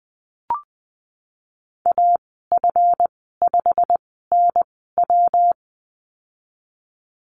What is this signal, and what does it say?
Morse code, heard as a keyed tone, about 20 wpm, spaced as for 15 wpm, 700 Hz. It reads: AF5NW
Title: 167 Lesson 27g - Number 7 - Call Signs 20wpm